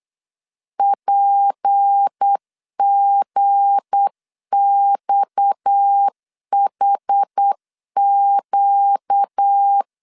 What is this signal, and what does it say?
Morse code, heard as a keyed tone, about 8 wpm, 800 Hz. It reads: PGXHQ
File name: LEZIONI / 7 GF9Z / 7 G 8wpm 3118